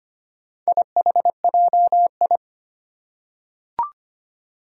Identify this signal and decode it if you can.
Morse code, heard as a keyed tone, about 25 wpm, 700 Hz. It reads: IHJI
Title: Four Challenging Letter-Number Combinations 25wpm